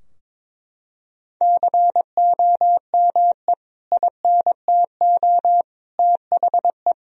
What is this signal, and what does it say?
Morse code, heard as a keyed tone, about 22 wpm, 700 Hz. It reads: COME INTO THE
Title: Sentences from Top 1000 Words - Review Part 2 - 22wpm